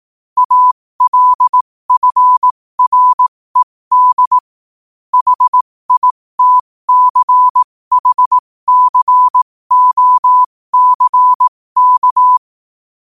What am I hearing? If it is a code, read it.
Morse code, heard as a keyed tone, about 18 wpm, spaced as for 15 wpm, 1000 Hz. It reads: ALFRED HITCHCOCK
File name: 15wpm